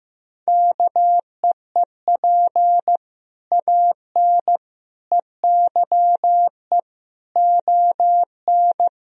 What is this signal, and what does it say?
Morse code, heard as a keyed tone, about 16 wpm, 700 Hz. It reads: KEEP AN EYE ON
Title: Sentences from Top 1000 Words - Review Part 3 - 15wpm